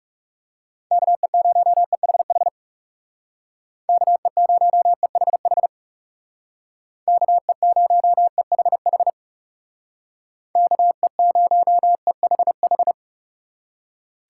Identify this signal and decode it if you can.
Morse code, heard as a keyed tone, about 37 wpm, 700 Hz. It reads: KE0EHH KE0EHH KE0EHH KE0EHH
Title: Speed Racing - USA 2x3 Call Signs 30wpm